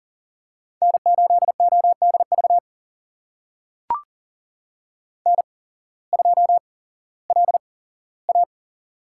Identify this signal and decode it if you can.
Morse code, heard as a keyed tone, about 40 wpm, spaced as for 19 wpm, 700 Hz. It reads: N8ODV N2LA
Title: USA Call Signs - 8x Character Spacing - 40wpm